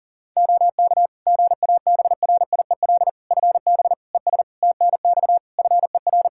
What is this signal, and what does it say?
Morse code, heard as a keyed tone, about 40 wpm, 700 Hz. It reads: OK GABRIEL FB ES TNX FER
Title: Practice QSO - Exchange 2 - Info - 40wpm